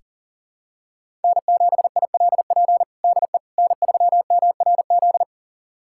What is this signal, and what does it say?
Morse code, heard as a keyed tone, about 37 wpm, 700 Hz. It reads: N7ILP DE N3MRZ
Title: USA Callsign DE Callsign - 5 Characters - 40wpm